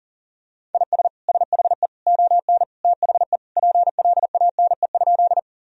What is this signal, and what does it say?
Morse code, heard as a keyed tone, about 40 wpm, 700 Hz. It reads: IS SHE ON THE PLANE?
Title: Sentences from Top 1000 Words - Review Part 3 - 40wpm